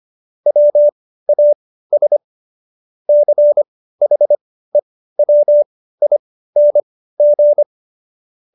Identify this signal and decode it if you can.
Morse code, heard as a keyed tone, about 25 wpm, spaced as for 15 wpm, 590 Hz. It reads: WAS CHEWING